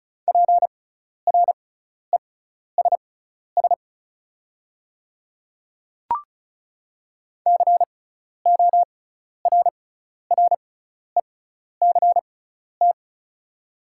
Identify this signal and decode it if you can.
Morse code, heard as a keyed tone, about 36 wpm, spaced as for 12 wpm, 700 Hz. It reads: PRESS CORRECT
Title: 6x Character Spacing - Top 500 Words - 35wpm